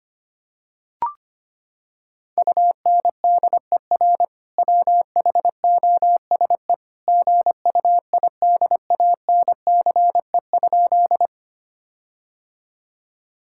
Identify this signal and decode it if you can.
Morse code, heard as a keyed tone, about 25 wpm, 700 Hz. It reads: UNDER WHOSE GUIDANCE?
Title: Sentences from Top 1600 Words 25wpm